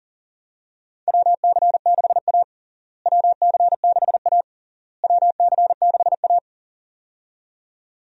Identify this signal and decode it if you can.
Morse code, heard as a keyed tone, about 37 wpm, 700 Hz. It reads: WC6A WC6A WC6A
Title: USA Call Signs - 4 Characters - Repeated 3x - 40wpm